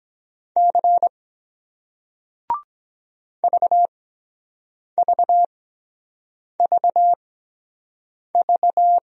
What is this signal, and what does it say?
Morse code, heard as a keyed tone, about 23 wpm, spaced as for 7 wpm, 700 Hz. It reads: C VVVV